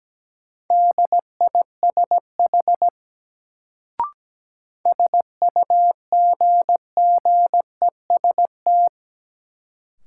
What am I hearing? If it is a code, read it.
Morse code, heard as a keyed tone, about 17 wpm, 700 Hz. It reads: DISH SUGGEST
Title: Sentences from Top 1300 Words - New Word Drill - 17wpm